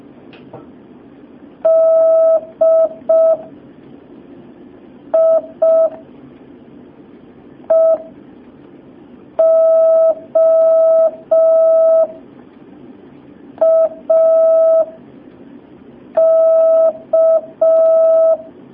Morse code, heard as a keyed tone, about 5 wpm, 650 Hz. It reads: D I E O A K
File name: jh8cbh / 2019cw